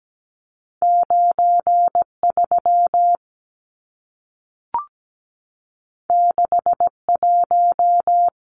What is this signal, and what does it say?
Morse code, heard as a keyed tone, about 17 wpm, 700 Hz. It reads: 93 61
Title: Numbers - 2-Digit 17wpm